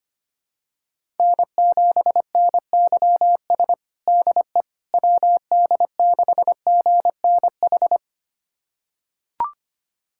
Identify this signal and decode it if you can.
Morse code, heard as a keyed tone, about 25 wpm, 700 Hz. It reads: N7NYS DE WD6GNH
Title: USA Callsign DE Callsign 25wpm